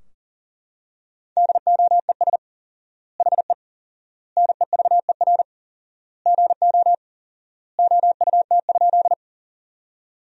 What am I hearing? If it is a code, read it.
Morse code, heard as a keyed tone, about 40 wpm, 700 Hz. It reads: DOES HE NEVER GO OUT?
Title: 4x Word-Spacing - Sentences from Top 300 Words - 40wpm